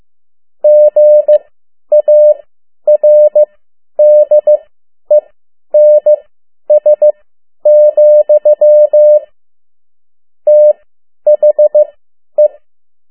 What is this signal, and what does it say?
Morse code, heard as a keyed tone, about 15 wpm, spaced as for 10 wpm, 600 Hz. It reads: GARDENS, THE
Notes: Around the World in 80 Days paragraph one Morse code 10 WPM